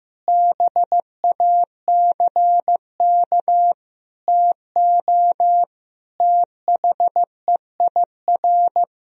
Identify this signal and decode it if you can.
Morse code, heard as a keyed tone, about 15 wpm, 700 Hz. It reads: BACK TO THEIR
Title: Sentences from Top 1000 Words - Review Part 1 - 15wpm